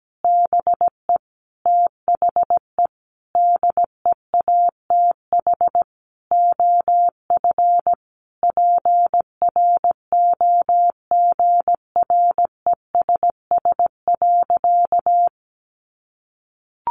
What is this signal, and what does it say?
Morse code, heard as a keyed tone, about 17 wpm, 700 Hz. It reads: BE THE DEATH OF PROGRESS.
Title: Sentences from Top 1400 Words 17wpm